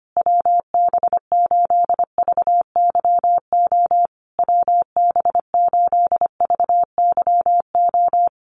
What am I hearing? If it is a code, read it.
Morse code, heard as a keyed tone, about 25 wpm, 700 Hz. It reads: WB8VYO WB8VYO
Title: USA Call Signs Repeated 3x 25wpm